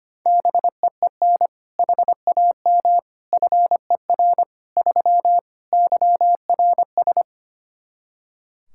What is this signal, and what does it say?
Morse code, heard as a keyed tone, about 25 wpm, 700 Hz. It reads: BEEN HAM FER 3 YRS
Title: K1USN SST QSO Practice 25wpm